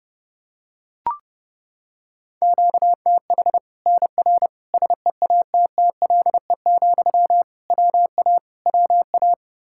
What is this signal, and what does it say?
Morse code, heard as a keyed tone, about 30 wpm, 700 Hz. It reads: QTH NR SEATTLE, WA WA
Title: Practice QSO - Exchange 1 - Beginning - 30wpm